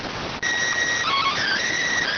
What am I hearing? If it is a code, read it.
SSTV (slow-scan television), PD 290.